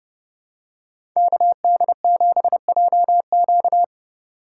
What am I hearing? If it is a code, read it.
Morse code, heard as a keyed tone, about 30 wpm, 700 Hz. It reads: KD7JQ
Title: USA Call Signs - 5 Characters - 4x Character Spacing - 30wpm